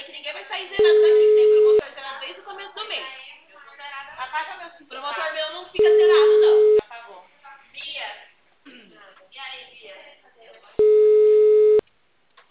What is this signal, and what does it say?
Ringback tone (Europe, 425 Hz), with speech.